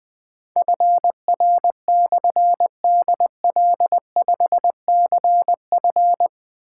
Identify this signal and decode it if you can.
Morse code, heard as a keyed tone, about 20 wpm, 700 Hz. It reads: FR/DL5CF
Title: Difficult Call Signs 20wpm